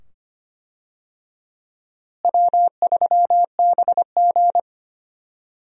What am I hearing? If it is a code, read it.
Morse code, heard as a keyed tone, about 25 wpm, 700 Hz. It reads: W3BG
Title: USA Call Signs - 4 Characters - Repeated 2x - 25wpm